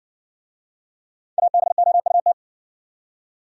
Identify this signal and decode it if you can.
Morse code, heard as a keyed tone, about 60 wpm, 700 Hz. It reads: ABOUT